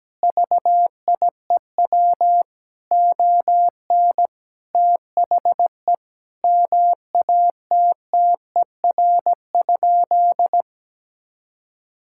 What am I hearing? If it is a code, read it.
Morse code, heard as a keyed tone, about 17 wpm, 700 Hz. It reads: VIEW ON THE MATTER?
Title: Sentences from Top 1000 Words - Review Part 6 - 17wpm